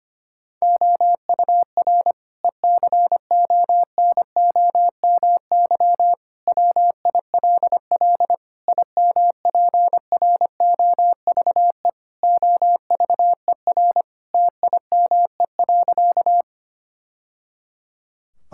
Morse code, heard as a keyed tone, about 25 wpm, 700 Hz. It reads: OUR ECONOMY WILL IMPROVE OVER TIME.
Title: Sentences from Top 1000 Words - Review Part 5 - 25wpm